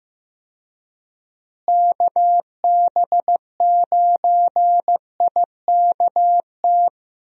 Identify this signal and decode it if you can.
Morse code, heard as a keyed tone, about 15 wpm, 700 Hz. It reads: KB9IKT
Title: USA Call Signs - 6 Characters - Repeated 3x - 15wpm